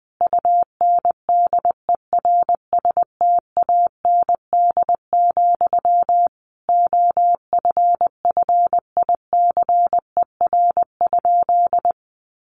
Morse code, heard as a keyed tone, about 20 wpm, 700 Hz. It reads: UNDERSTAND, OFFICER?
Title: Sentences from Top 1200 Words 20wpm